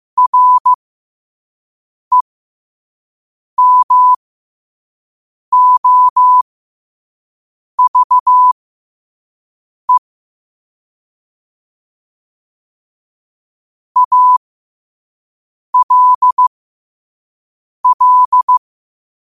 Morse code, heard as a keyed tone, about 15 wpm, spaced as for 5 wpm, 1000 Hz. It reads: REMOVE ALL
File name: qotdfiles / 5wpm